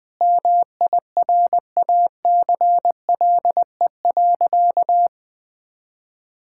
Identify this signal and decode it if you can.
Morse code, heard as a keyed tone, about 21 wpm, 700 Hz. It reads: MIRACLE.
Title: Sentences with 7-Word Phrases - Spoken Beginning - 20wpm